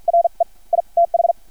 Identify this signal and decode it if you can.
Morse code, heard as a keyed tone, about 50 wpm, spaced as for 41 wpm, 680 Hz. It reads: RE I TH
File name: archived / CRSnet